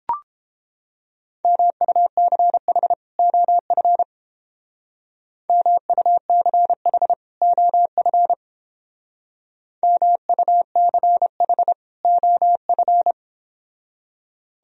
Morse code, heard as a keyed tone, about 30 wpm, 700 Hz. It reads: MUCH OF MUCH OF MUCH OF